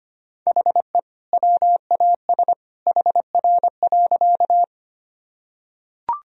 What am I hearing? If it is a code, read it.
Morse code, heard as a keyed tone, about 25 wpm, 700 Hz. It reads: HE WAS HR.
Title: Sentences from Top 200 Words and Abbreviations 25wpm